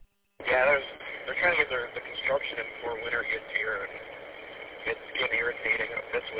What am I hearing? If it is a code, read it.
Emergency Alert System (SAME) tones (header bursts), heard as harsh buzzing bursts, with speech. It is heard from 1.0 to 6.0 s.
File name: w8cmn- -combined